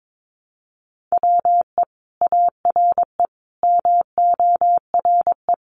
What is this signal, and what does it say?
Morse code, heard as a keyed tone, about 22 wpm, 700 Hz. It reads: WE ARE MORE
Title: Sentences from Top 500 Words 22wpm